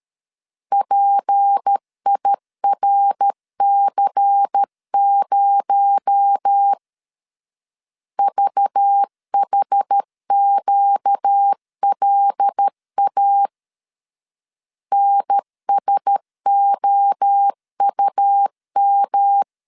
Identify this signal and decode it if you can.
Morse code, heard as a keyed tone, about 13 wpm, 800 Hz. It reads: PIRC0 VHQLA NSOUM
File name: LEZIONI / 6 PKX0 / 6 PKX0 12wpm 2231